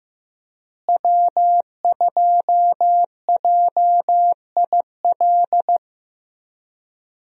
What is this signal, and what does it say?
Morse code, heard as a keyed tone, about 15 wpm, 700 Hz. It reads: W2JIL
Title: USA Call Signs - 5 Characters - Repeated 3x - 15wpm